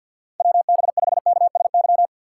Morse code, heard as a keyed tone, about 53 wpm, 700 Hz. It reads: WB5KSY
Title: USA Call Signs - 6 Characters - 4x Character Spacing - 50wpm